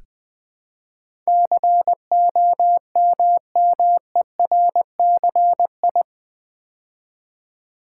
Morse code, heard as a keyed tone, about 20 wpm, 700 Hz. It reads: COMMERCI
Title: Build A Word - 4 Letters At A Time - 10-Letter Words - 20wpm